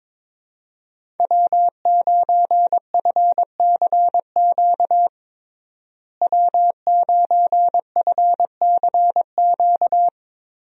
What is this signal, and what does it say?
Morse code, heard as a keyed tone, about 22 wpm, 700 Hz. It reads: W9FCQ W9FCQ
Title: USA Call Signs - 5 Characters - Repeated 3x - 22wpm